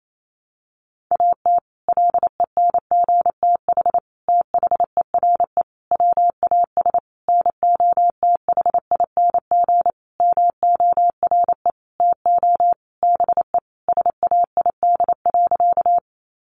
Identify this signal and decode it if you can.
Morse code, heard as a keyed tone, about 28 wpm, 700 Hz. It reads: AT LENGTH THERE WAS NOTHING MORE TO BE SAID.
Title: Sentences from Top 1000 Words 28wpm